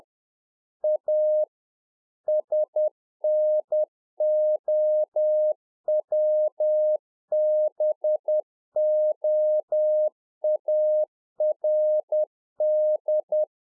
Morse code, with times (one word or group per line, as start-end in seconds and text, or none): 0.8-1.4 A
2.3-13.4 SNOWBOARD